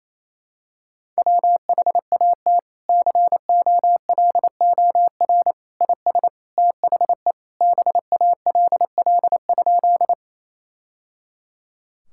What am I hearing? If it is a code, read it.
Morse code, heard as a keyed tone, about 28 wpm, 700 Hz. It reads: WHAT COLOR IS THE BALL?
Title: Sentences from Top 1000 Words - Review Part 6 - 28wpm